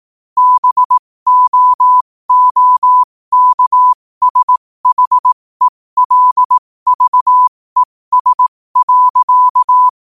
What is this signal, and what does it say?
Morse code, heard as a keyed tone, about 18 wpm, spaced as for 15 wpm, 1000 Hz. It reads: BOOKSHELVES.
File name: qotdfiles / 15wpm